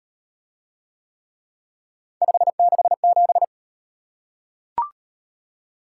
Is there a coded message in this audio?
Yes: Morse code from 2.2 to 3.4 s, heard as a keyed tone, about 38 wpm, 700 Hz. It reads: 567